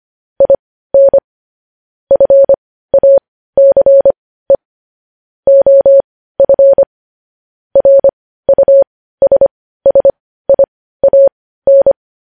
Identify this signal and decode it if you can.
Morse code, heard as a keyed tone, about 25 wpm, spaced as for 15 wpm, 550 Hz. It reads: IN FACE OF RUSSIAN
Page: News Headlines in Morse Code at 15 WPM